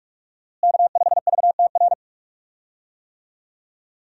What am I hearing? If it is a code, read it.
Morse code, heard as a keyed tone, about 46 wpm, 700 Hz. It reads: K5VTR
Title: USA Call Signs - 5 Characters - Repeated 2x - 45wpm